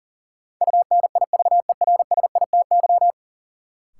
Morse code, heard as a keyed tone, about 40 wpm, 700 Hz. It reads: UNIVERSITY
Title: Sets of 4 Words - New Word Drill - 40wpm